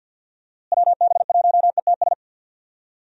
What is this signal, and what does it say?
Morse code, heard as a keyed tone, about 46 wpm, 700 Hz. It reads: WB1ETS
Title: USA Call Signs - 6 Characters - Repeated 2x - 50wpm